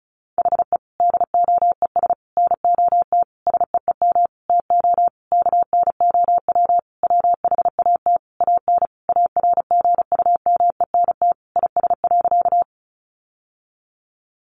Morse code, heard as a keyed tone, about 34 wpm, 700 Hz. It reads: HE DOES NOT SEEM TO KNOW WHAT AN ARGUMENT IS.
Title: Sentences from Top 1400 Words 35wpm